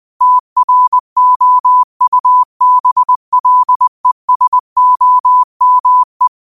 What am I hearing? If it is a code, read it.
Morse code, heard as a keyed tone, about 20 wpm, 1000 Hz. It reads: TROUBLESOME